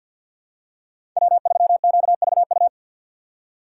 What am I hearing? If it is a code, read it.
Morse code, heard as a keyed tone, about 48 wpm, 690 Hz. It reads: W3QVU
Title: USA Call Signs - 5 Characters - 2x Character Spacing - 50wpm